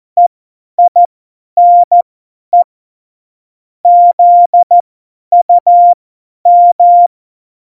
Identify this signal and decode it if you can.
Morse code, heard as a keyed tone, about 14 wpm, spaced as for 10 wpm, 700 Hz. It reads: EINE ZUM
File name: Morsekurs / Witze / Witz 05. Juni 2018, 50 F70